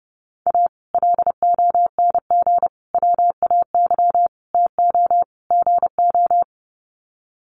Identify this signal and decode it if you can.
Morse code, heard as a keyed tone, about 28 wpm, 700 Hz. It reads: A LONG WAY TO GO
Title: Sets of 5 Words 30wpm